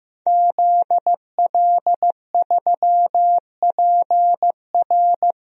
Morse code, heard as a keyed tone, about 15 wpm, 700 Hz. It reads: ZL3PR
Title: DXCC-100 Call Signs - 5 Characters - 15wpm